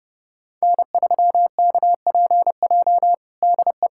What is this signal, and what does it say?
Morse code, heard as a keyed tone, about 32 wpm, 700 Hz. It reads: N3KPJ DE
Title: USA Callsign DE Callsign 30wpm